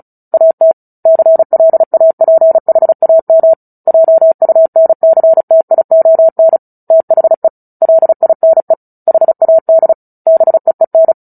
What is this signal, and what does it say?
Morse code, heard as a keyed tone, about 34 wpm, 650 Hz. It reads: AT CLAPHAM JUNCTION THE LINE HAD BEEN
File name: Book Two / 09-Chapter 9 - 35 WPM